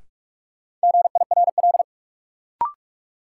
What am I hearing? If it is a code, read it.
Morse code, heard as a keyed tone, about 47 wpm, 700 Hz. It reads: GIRL
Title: Sentences from Top 400 Words - New Word Drill - 45wpm